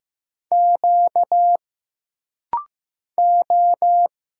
Morse code, heard as a keyed tone, about 15 wpm, 700 Hz. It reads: Q O
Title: Single Letter-Number Alt - Rapid-Fire 15wpm